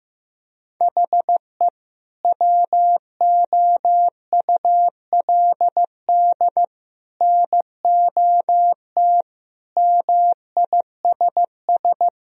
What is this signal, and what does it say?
Morse code, heard as a keyed tone, about 15 wpm, 700 Hz. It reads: HE WOULD NOT MISS
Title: Sentences from Top 1000 Words - Review Part 3 - 15wpm